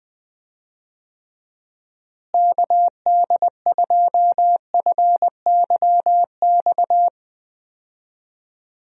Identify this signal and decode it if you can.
Morse code, heard as a keyed tone, about 20 wpm, 700 Hz. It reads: KD2FYX
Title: USA Call Signs - 6 Characters - Repeated 2x - 20wpm